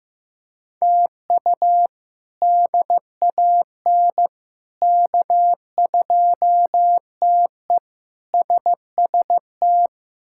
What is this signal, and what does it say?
Morse code, heard as a keyed tone, about 14 wpm, 700 Hz. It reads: TU DAN K2TE SST
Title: K1USN SST QSO Practice - 15wpm